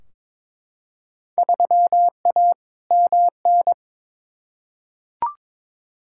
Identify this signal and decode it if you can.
Morse code, heard as a keyed tone, about 22 wpm, 700 Hz. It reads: 3A MN
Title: Field Day Exchanges - 22wpm